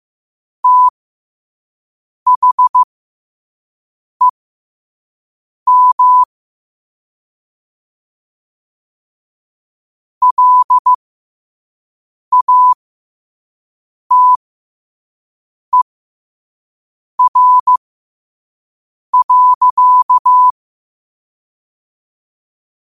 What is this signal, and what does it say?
Morse code, heard as a keyed tone, about 15 wpm, spaced as for 5 wpm, 1000 Hz. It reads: THEM LATER.
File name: qotdfiles / 5wpm